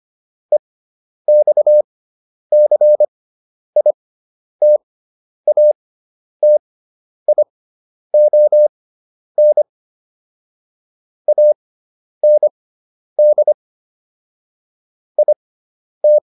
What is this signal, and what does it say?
Morse code, heard as a keyed tone, about 25 wpm, spaced as for 10 wpm, 600 Hz. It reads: EXCITATION AND IT